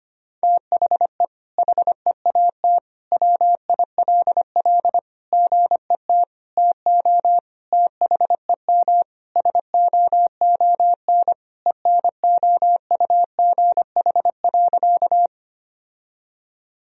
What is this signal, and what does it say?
Morse code, heard as a keyed tone, about 25 wpm, 700 Hz. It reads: THE HEAT WILL GET TO THEM SOON ENOUGH.
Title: Sentences from Top 1000 Words - Review Part 3 - 25wpm